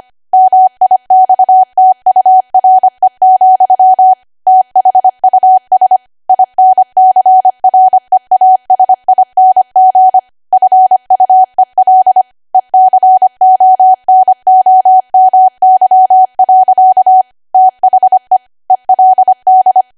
Morse code, heard as a keyed tone, about 25 wpm, 750 Hz. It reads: MIXTURE, THUS INCREASING FUEL ECONOMY. THE ELD
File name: Archive / 25 WPM / 25WPM